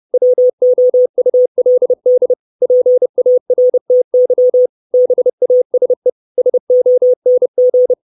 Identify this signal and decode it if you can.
Morse code, heard as a keyed tone, about 30 wpm, 500 Hz. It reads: WOULD PARTY BASE SONG
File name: 632 Words 30WPM